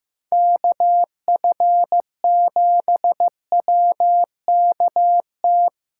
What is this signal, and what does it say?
Morse code, heard as a keyed tone, about 18 wpm, 700 Hz. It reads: KF7WKT
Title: Speed Racing - USA Call Signs 15wpm